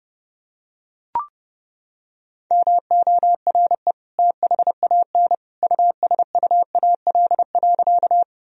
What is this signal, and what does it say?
Morse code, heard as a keyed tone, about 30 wpm, 700 Hz. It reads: MORE THAN USUAL.
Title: Sentences from Top 1000 Words - Review Part 2 - 30wpm